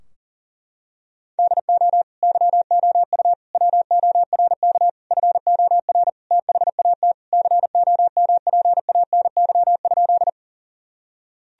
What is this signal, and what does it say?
Morse code, heard as a keyed tone, about 40 wpm, 700 Hz. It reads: DO YOU WORK FOR THAT COMPANY?
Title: Sentences from Top 1000 Words - Review Part 3 - 40wpm